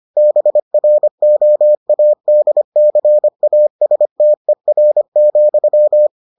Morse code, heard as a keyed tone, about 25 wpm, 600 Hz. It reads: BROADCASTER,